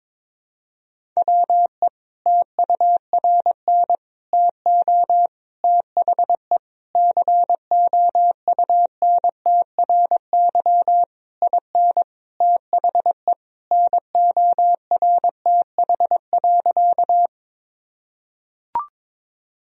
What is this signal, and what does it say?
Morse code, heard as a keyed tone, about 22 wpm, 700 Hz. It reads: WE TURN TO THE COUNTRY IN THE NORTH.
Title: Sentences from Top 1000 Words - Review Part 1 - 22wpm